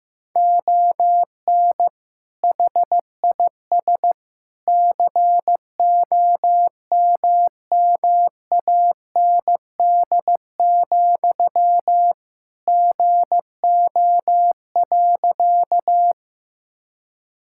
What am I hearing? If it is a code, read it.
Morse code, heard as a keyed tone, about 15 wpm, 700 Hz. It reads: ON HIS COMMAND, GO.
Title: Sentences from Top 1600 Words 15wpm